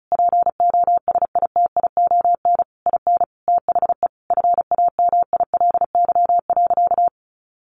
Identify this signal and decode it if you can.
Morse code, heard as a keyed tone, about 34 wpm, 700 Hz. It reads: POSITION IN THE FAMILY.
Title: Sentences from Top 1200 Words 35wpm